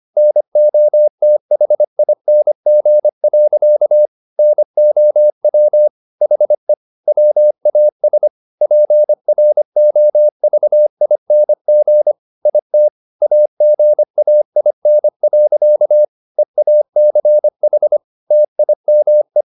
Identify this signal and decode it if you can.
Morse code, heard as a keyed tone, about 25 wpm, 600 Hz. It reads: NOTHING. NOW HE WAS PROVING IT AGAIN. EACH TIME
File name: the old man and the sea part 1 25wpm